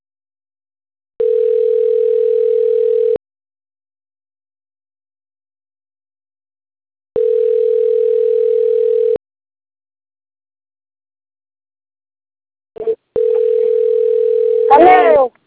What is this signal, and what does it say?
Ringback tone (North America), 440 + 480 Hz.